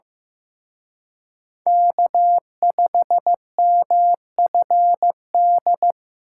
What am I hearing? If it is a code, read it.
Morse code, heard as a keyed tone, about 18 wpm, 700 Hz. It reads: K5MFD
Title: Speed Racing - USA 1x3 Call Signs 15wpm